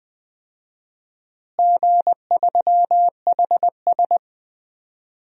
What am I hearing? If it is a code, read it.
Morse code, heard as a keyed tone, about 20 wpm, 700 Hz. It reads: G3HS
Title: DXCC-100 Call Signs - 4 Characters - Repeated 3x - 20wpm